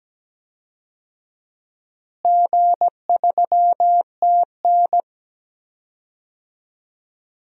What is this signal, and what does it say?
Morse code, heard as a keyed tone, about 17 wpm, 700 Hz. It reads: G3TN
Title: DXCC-100 Call Signs - 4 Characters - Repeated 3x - 17wpm